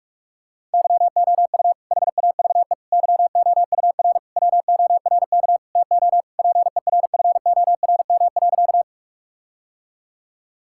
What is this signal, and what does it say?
Morse code, heard as a keyed tone, about 46 wpm, 700 Hz. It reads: YOU HAVE YOUR WORK TO PERFORM.
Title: Sentences from Top 1000 Words - Review Part 5 - 45wpm